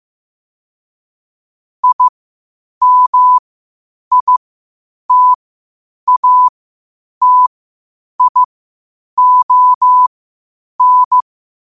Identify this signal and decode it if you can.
Morse code, heard as a keyed tone, about 15 wpm, spaced as for 8 wpm, 1000 Hz. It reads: IMITATION
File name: qotdfiles / 8wpm